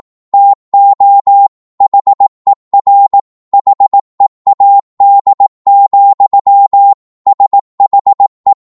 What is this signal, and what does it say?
Morse code, heard as a keyed tone, about 18 wpm, 800 Hz. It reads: TO HER HEAD, SHE
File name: 18wpm